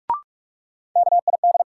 Morse code, heard as a keyed tone, about 46 wpm, 700 Hz. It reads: KID
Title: Sentences from Top 1600 Words - New Word Drill - 45wpm